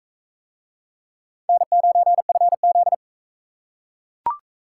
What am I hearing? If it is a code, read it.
Morse code, heard as a keyed tone, about 39 wpm, 700 Hz. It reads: N9FZ